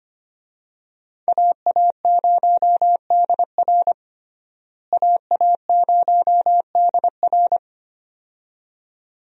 Morse code, heard as a keyed tone, about 25 wpm, 700 Hz. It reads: AA0DR AA0DR
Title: USA Call Signs - 5 Characters - Repeated 2x - 25wpm